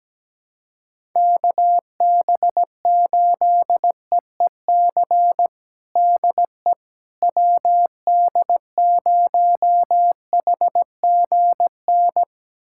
Morse code, heard as a keyed tone, about 17 wpm, 700 Hz. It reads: KB8EEC DE WD0HGN
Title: USA Callsign DE Callsign - 6 Characters - 17wpm